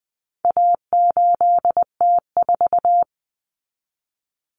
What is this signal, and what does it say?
Morse code, heard as a keyed tone, about 20 wpm, 700 Hz. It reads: A8T4
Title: Common Cut Numbers - 4-Digit 20wpm